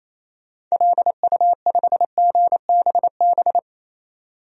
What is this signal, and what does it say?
Morse code, heard as a keyed tone, about 28 wpm, 700 Hz. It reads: LU5GBB
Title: DXCC-100 Call Signs - 6 Characters - 28wpm